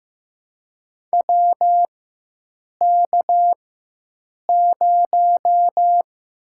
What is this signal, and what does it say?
Morse code, heard as a keyed tone, about 15 wpm, 700 Hz. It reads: W K 0
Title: USA Call Signs - 4 Characters - 4x Character Spacing - 15wpm